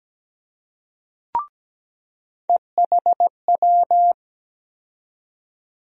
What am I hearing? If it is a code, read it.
Morse code, heard as a keyed tone, about 17 wpm, 700 Hz. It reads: EHW
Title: USA Call Sign Postfixes - 3 Letters 17wpm